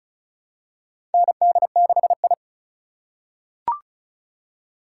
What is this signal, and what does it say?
Morse code, heard as a keyed tone, about 36 wpm, 700 Hz. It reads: ND6I
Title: USA Call Signs - 4 Characters - 2x Character Spacing - 35wpm